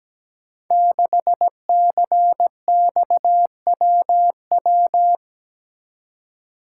Morse code, heard as a keyed tone, about 17 wpm, 700 Hz. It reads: 6CXWW
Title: Code Groups - 5 Letter:Numbers - 17wpm